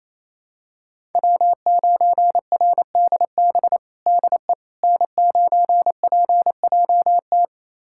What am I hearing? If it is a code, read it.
Morse code, heard as a keyed tone, about 27 wpm, 700 Hz. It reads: W9RDB DE N9PJT
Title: USA Callsign DE Callsign - 5 Characters - 28wpm